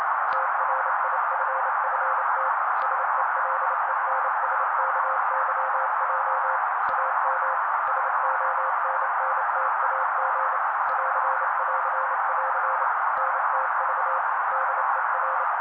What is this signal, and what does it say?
Morse code, heard as a keyed tone, about 28 wpm, 530 Hz. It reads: TRIFFT SELENSKYJ AM SONNTAG <AR><AR><AR> NTV DER
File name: daily / 27wpm noise